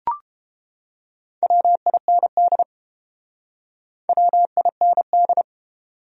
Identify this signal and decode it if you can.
Morse code, heard as a keyed tone, about 29 wpm, 700 Hz. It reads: WIND WIND